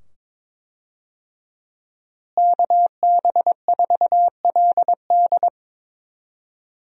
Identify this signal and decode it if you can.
Morse code, heard as a keyed tone, about 22 wpm, 700 Hz. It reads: KB4LD
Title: USA Call Signs - 5 Characters - Repeated 2x - 22wpm